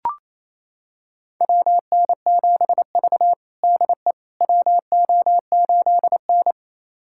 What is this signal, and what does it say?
Morse code, heard as a keyed tone, about 28 wpm, 700 Hz. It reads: WN7V DE WO8N
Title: USA Callsign DE Callsign - 4 Characters - 28wpm